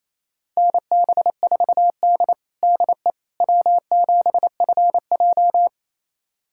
Morse code, heard as a keyed tone, about 28 wpm, 700 Hz. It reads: NB4D DE W7FJ
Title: USA Callsign DE Callsign - 4 Characters - 28wpm